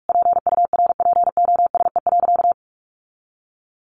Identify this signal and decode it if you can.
Morse code, heard as a keyed tone, about 44 wpm, 700 Hz. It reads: PURPOSE.
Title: Sentences from Top 1400 Words 45wpm